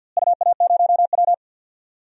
Morse code, heard as a keyed tone, about 51 wpm, 700 Hz. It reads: UA0W
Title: DXCC-100 Call Signs - 4 Characters - Repeated 3x - 50wpm